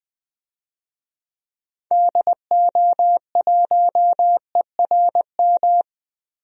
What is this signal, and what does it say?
Morse code, heard as a keyed tone, about 21 wpm, 700 Hz. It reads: DO1ERM
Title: DXCC-100 Call Signs - 6 Characters - Repeated 3x - 20wpm